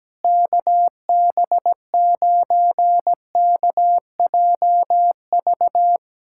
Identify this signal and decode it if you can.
Morse code, heard as a keyed tone, about 17 wpm, 700 Hz. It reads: KB9KJV
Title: USA Callsign DE Callsign - 6 Characters - 17wpm